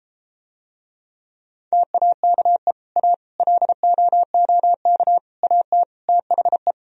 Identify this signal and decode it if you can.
Morse code, heard as a keyed tone, about 33 wpm, 700 Hz. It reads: TAKE A LOOK AT THE